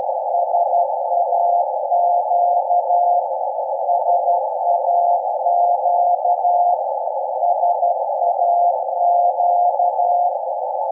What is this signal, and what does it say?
Morse code, heard as a keyed tone, about 12 wpm, 700 Hz, with noise. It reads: SAQ SAQ SAQ
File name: rec / 24-Dec-2012-0856 0.020MHz